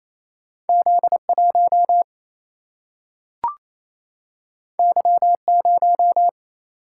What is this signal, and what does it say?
Morse code, heard as a keyed tone, about 28 wpm, 700 Hz. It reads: Z 1 Y 0
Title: Single-Letter Single-Number 28wpm